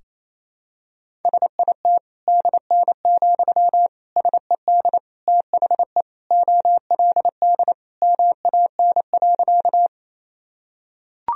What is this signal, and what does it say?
Morse code, heard as a keyed tone, about 28 wpm, 700 Hz. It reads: SIT DN, SED THE OLD MAN.
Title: Sentences from Top 500 Words and Abbreviations 28wpm